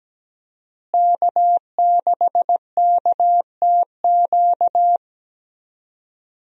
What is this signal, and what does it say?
Morse code, heard as a keyed tone, about 17 wpm, 700 Hz. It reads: K6KTQ
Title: USA Call Signs - 5 Characters - Repeated 3x - 17wpm